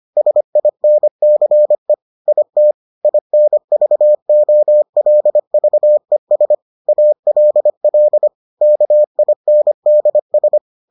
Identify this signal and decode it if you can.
Morse code, heard as a keyed tone, about 25 wpm, 600 Hz. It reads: SINCE IT INVOLVES ALL KINDS